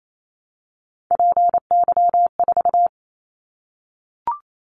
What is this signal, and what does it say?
Morse code, heard as a keyed tone, about 28 wpm, 700 Hz. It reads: PY4
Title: Callsign Prefixes for Top 50 DXCC - 28wpm